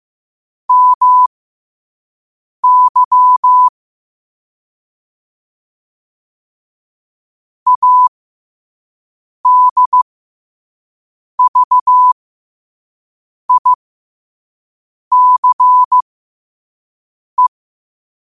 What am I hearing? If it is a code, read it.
Morse code, heard as a keyed tone, about 15 wpm, spaced as for 5 wpm, 1000 Hz. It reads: MY ADVICE
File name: qotdfiles / 5wpm